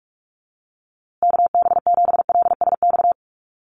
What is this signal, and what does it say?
Morse code, heard as a keyed tone, about 44 wpm, 700 Hz. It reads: KB7LSX
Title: USA Call Signs Repeated 2x 45wpm